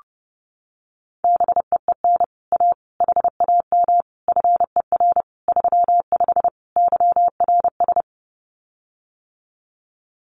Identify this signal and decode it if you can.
Morse code, heard as a keyed tone, about 28 wpm, 700 Hz. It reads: BEEN A HAM FER 35 YRS
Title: Common QSO Phrases Encore 30wpm